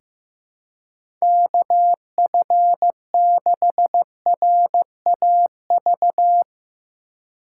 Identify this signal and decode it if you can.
Morse code, heard as a keyed tone, about 15 wpm, 700 Hz. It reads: KF6RAV
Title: USA Call Signs - 3x Character Spacing - 15wpm